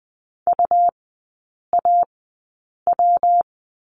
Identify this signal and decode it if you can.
Morse code, heard as a keyed tone, about 20 wpm, spaced as for 8 wpm, 700 Hz. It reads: UAW
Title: 127 Lesson 21e - Number 6 - Three Characters 20wpm